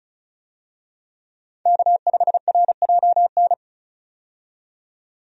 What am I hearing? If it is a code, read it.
Morse code, heard as a keyed tone, about 36 wpm, 700 Hz. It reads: K5RJN
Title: USA Call Signs - 5 Characters - 35wpm